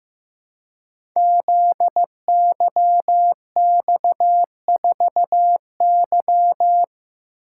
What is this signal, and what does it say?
Morse code, heard as a keyed tone, about 15 wpm, 700 Hz. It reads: ZYX4Y
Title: Code Groups - 5 Letter:Numbers - 15wpm